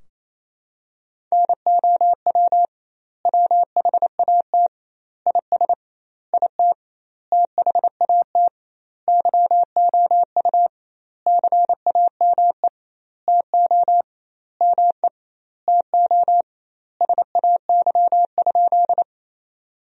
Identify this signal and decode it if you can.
Morse code, heard as a keyed tone, about 28 wpm, 700 Hz. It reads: NOW WHAT IS IT THAT YOU CAME TO ME TO SAY?
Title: 2x Word-Spacing - Sentences from Top 200 Words - 28wpm